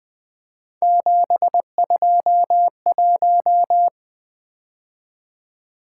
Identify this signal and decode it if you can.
Morse code, heard as a keyed tone, about 23 wpm, 700 Hz. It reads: 721
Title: Speed-Racing - Three Letter-Number Combinations - 20wpm